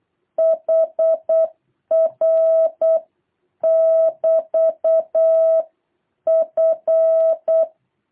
Morse code, heard as a keyed tone, about 8 wpm, 660 Hz. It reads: HR <BT> F
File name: jh8cbh / 2023cw